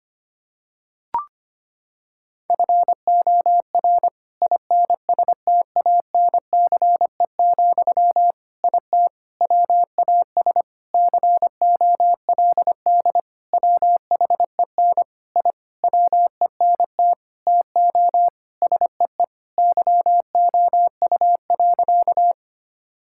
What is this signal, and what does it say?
Morse code, heard as a keyed tone, about 25 wpm, 700 Hz. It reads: FOR INSTANCE, IT WAS COLD WHEN I WENT TO SEE YOU.
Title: Sentences from Top 1000 Words - Review Part 5 - 25wpm